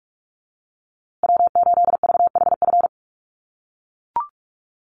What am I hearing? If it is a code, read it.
Morse code, heard as a keyed tone, about 44 wpm, 700 Hz. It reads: W8VHF
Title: USA Call Signs Repeated 2x 45wpm